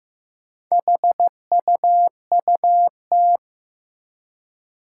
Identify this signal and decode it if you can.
Morse code, heard as a keyed tone, about 15 wpm, 700 Hz. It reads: HUUT
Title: Four Challenging Letter-Number Combinations 15wpm